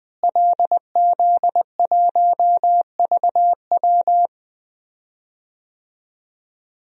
Morse code, heard as a keyed tone, about 20 wpm, 700 Hz. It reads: LZ1VW
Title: DXCC-100 Call Signs - 5 Characters - Repeated 3x - 20wpm